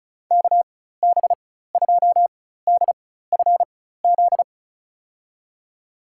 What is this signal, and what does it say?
Morse code, heard as a keyed tone, about 36 wpm, spaced as for 16 wpm, 700 Hz. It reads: KB2DFZ
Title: USA Call Signs - 6 Characters - 4x Character Spacing - 35wpm